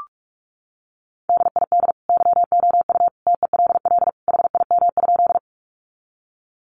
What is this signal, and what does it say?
Morse code, heard as a keyed tone, about 44 wpm, 700 Hz. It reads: DID YOU TELL HIM?
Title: Sentences from Top 200 Words 45wpm